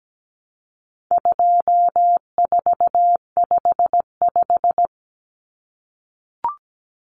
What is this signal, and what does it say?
Morse code, heard as a keyed tone, about 17 wpm, 700 Hz. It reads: 2455
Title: Common Cut Numbers - 4-Digit 17wpm